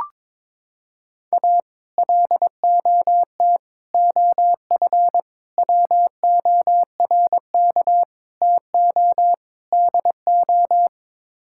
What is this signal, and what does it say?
Morse code, heard as a keyed tone, about 22 wpm, 700 Hz. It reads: A LOT OF WORK TO DO
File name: practice / 22wpm / Sets of 6 Words - 22wpm